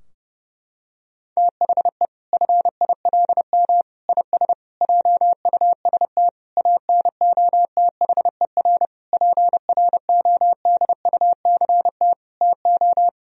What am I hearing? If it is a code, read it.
Morse code, heard as a keyed tone, about 30 wpm, 700 Hz. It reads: THE FILM IS JUST ANOTHER PRODUCT TO
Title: Sentences from Top 1000 Words - Review Part 3 - 30wpm